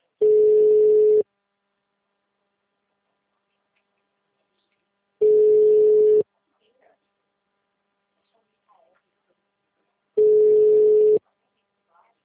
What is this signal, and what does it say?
Ringback tone (Europe, 425 Hz), with speech.